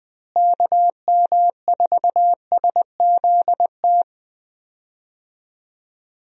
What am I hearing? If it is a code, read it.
Morse code, heard as a keyed tone, about 20 wpm, 700 Hz. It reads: KM4SZT
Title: USA Call Signs - 6 Characters - Repeated 3x - 20wpm